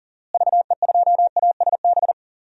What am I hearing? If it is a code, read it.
Morse code, heard as a keyed tone, about 38 wpm, 700 Hz. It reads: VE2ASB
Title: DXCC-100 Call Signs - 6 Characters - Repeated 3x - 40wpm